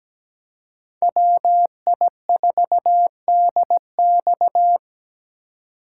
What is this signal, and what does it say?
Morse code, heard as a keyed tone, about 20 wpm, 700 Hz. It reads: WI4DX
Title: Speed Racing - USA Call Signs 17wpm